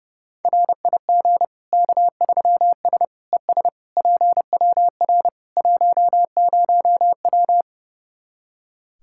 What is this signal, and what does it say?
Morse code, heard as a keyed tone, about 32 wpm, 700 Hz. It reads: RIG K3S ES PWR 10W
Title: Practice QSO - Exchange 2 - Info - 30wpm